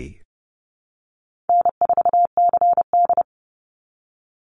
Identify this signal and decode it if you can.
Morse code, heard as a keyed tone, about 30 wpm, 700 Hz. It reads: N4CD
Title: Easy Call Signs Repeated 3x - 30wpm